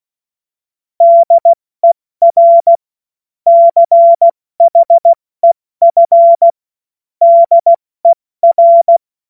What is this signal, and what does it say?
Morse code, heard as a keyed tone, about 16 wpm, 680 Hz. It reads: DER CHEF DER
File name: Morsekurs / Witze / Witz 27 Februar 2024 70 F80